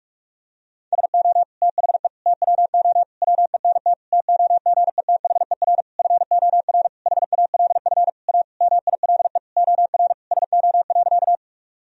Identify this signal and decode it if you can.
Morse code, heard as a keyed tone, about 46 wpm, 700 Hz. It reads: SO THE TWO WENT TOGETHER FOR HALF A MILE OR SO.
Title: Sentences from Top 1000 Words - Review Part 1 - 45wpm